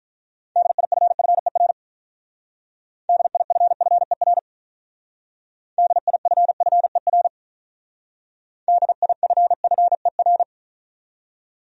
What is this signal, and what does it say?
Morse code, heard as a keyed tone, about 45 wpm, 700 Hz. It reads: DIFFER DIFFER DIFFER DIFFER